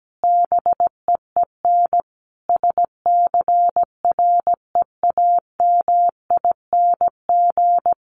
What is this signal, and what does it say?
Morse code, heard as a keyed tone, about 17 wpm, 700 Hz. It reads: BEEN SCREAMING
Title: A Sound Of Thunder 17wpm